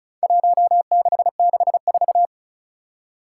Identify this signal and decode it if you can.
Morse code, heard as a keyed tone, about 36 wpm, 700 Hz. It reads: 1664
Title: Four Challenging Letter-Number Combinations 35wpm